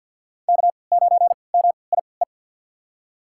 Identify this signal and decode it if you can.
Morse code, heard as a keyed tone, about 48 wpm, spaced as for 28 wpm, 700 Hz. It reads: K9MIE
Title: USA Call Signs - 5 Characters - 3x Character Spacing - 50wpm